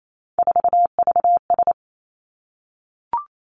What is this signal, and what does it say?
Morse code, heard as a keyed tone, about 28 wpm, 700 Hz. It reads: 4VS
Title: Three Challenging Letter-Number Combinations 28wpm